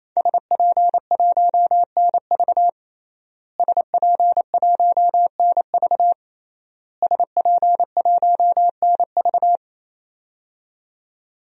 Morse code, heard as a keyed tone, about 28 wpm, 700 Hz. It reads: SP1NV SP1NV SP1NV
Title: DXCC-100 Call Signs - 5 Characters - Repeated 3x - 28wpm